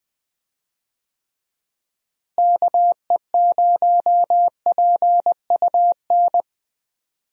Morse code, heard as a keyed tone, about 20 wpm, 700 Hz. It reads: KE0PUN
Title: USA Call Signs - 6 Characters - Repeated 2x - 20wpm